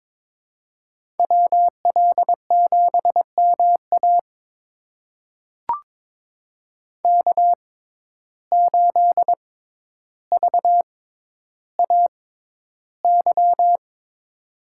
Morse code, heard as a keyed tone, about 22 wpm, spaced as for 11 wpm, 700 Hz. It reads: WL7MA K8VAY
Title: USA Call Signs - 5 Characters - 6x Character Spacing - 22wpm